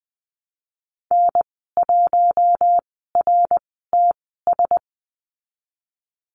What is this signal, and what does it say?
Morse code, heard as a keyed tone, about 20 wpm, spaced as for 15 wpm, 700 Hz. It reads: N1RTS
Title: 069 Lesson 12g - Letter d - Call Signs 20wpm